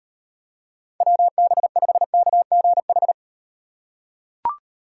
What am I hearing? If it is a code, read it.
Morse code, heard as a keyed tone, about 38 wpm, 700 Hz. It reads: WB5KGH